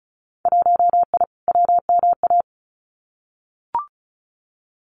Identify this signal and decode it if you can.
Morse code, heard as a keyed tone, about 36 wpm, 700 Hz. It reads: 1I WMA
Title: Winter Field Day Exchanges Encore 35wpm